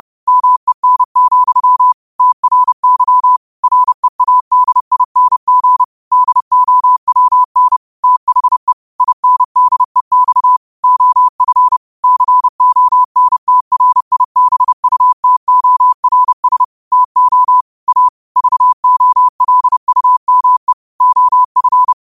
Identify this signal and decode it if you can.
Morse code, heard as a keyed tone, about 30 wpm, 1000 Hz. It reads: MEN, TRY READING DOWN THE INDEX OF CONTRIBUTORS TO A VOLUME OF